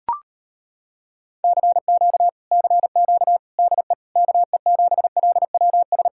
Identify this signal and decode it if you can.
Morse code, heard as a keyed tone, about 33 wpm, 700 Hz. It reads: CQ CQ DE KE7LWS